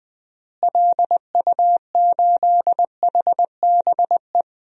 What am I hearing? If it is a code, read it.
Morse code, heard as a keyed tone, about 20 wpm, 700 Hz. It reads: LU8HBE
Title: DXCC-100 Call Signs - 6 Characters - Repeated 3x - 20wpm